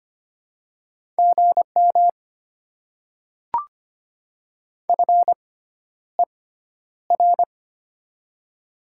Morse code, heard as a keyed tone, about 25 wpm, spaced as for 11 wpm, 700 Hz. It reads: GM FER
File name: Top 90 QSO Elements - 6x Character Spacing - 25wpm